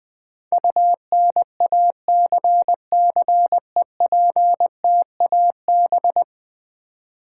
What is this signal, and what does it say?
Morse code, heard as a keyed tone, about 20 wpm, 700 Hz. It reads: UNACCEPTAB
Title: Build A Word - 5 Letters At A Time - 12-Letter Words - 20wpm